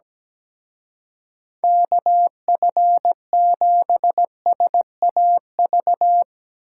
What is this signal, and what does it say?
Morse code, heard as a keyed tone, about 18 wpm, 700 Hz. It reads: KF7SAV
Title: Speed Racing - USA 2x3 Call Signs 17wpm